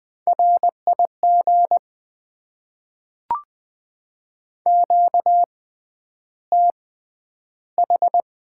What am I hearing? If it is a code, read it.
Morse code, heard as a keyed tone, about 20 wpm, spaced as for 10 wpm, 700 Hz. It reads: RIG QTH
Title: Top 20 QSO Elements - 6x Character Spacing - 20wpm